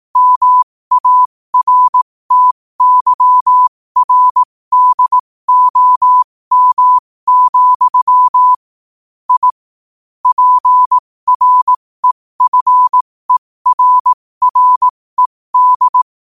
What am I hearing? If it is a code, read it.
Morse code, heard as a keyed tone, about 18 wpm, spaced as for 15 wpm, 1000 Hz. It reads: MARTYRDOM, I PREFERRED
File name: qotdfiles / 15wpm